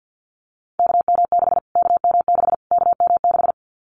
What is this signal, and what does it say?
Morse code, heard as a keyed tone, about 50 wpm, 700 Hz. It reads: KM6 KM6 KM6
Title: Callsign Prefixes for Top 25 DXCC - Repeated 3x - 50wpm